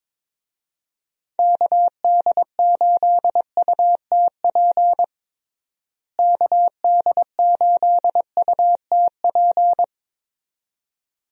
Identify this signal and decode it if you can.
Morse code, heard as a keyed tone, about 22 wpm, 700 Hz. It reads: KD8UTP KD8UTP
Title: USA Call Signs - 6 Characters - Repeated 2x - 22wpm